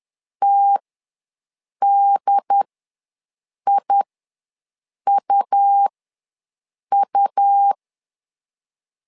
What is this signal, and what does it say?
Morse code, heard as a keyed tone, about 10 wpm, spaced as for 6 wpm, 800 Hz. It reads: TDIUU
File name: LEZIONI / 2 UHDN / 2 UHD ws 10wpm 2058